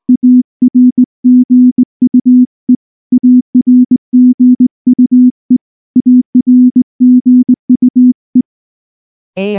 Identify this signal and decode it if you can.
Morse code, heard as a keyed tone, about 19 wpm, 250 Hz. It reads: ARGUE ARGUE ARGUE
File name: morse-training-20WPM